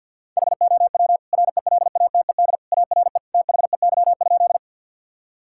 Morse code, heard as a keyed tone, about 50 wpm, 700 Hz. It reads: HOW RELATED ARE THEY?
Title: Sentences from Top 1000 Words - Review Part 5 - 50wpm